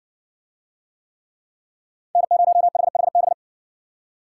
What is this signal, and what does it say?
Morse code, heard as a keyed tone, about 60 wpm, 700 Hz. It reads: N0HHB